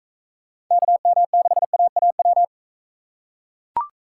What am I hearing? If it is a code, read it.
Morse code, heard as a keyed tone, about 40 wpm, 700 Hz. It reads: KM6AAW